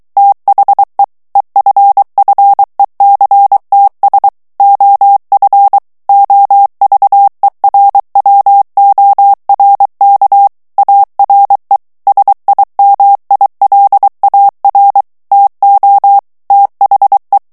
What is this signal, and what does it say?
Morse code, heard as a keyed tone, about 23 wpm, 800 Hz. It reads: THE EFFECTS OF OVERWORK ARE SIMILAR TO THE
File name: morse mp3 / ANGLAIS texte clair 1 vitesse 22 mots minute